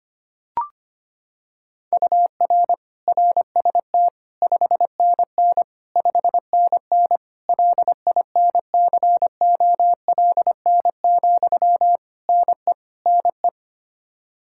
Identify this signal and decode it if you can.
Morse code, heard as a keyed tone, about 25 wpm, 700 Hz. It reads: UR RST 5NN 5NN LINCOLN, NE NE
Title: Practice QSO - Exchange 1 - Beginning - 25wpm